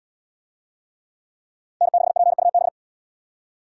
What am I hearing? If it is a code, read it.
Morse code, heard as a keyed tone, about 78 wpm, 700 Hz. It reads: N6PUB